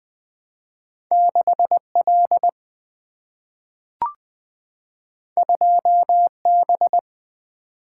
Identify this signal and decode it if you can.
Morse code, heard as a keyed tone, about 20 wpm, 700 Hz. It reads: 6 L 2 B
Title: Single-Letter Single-Number 20wpm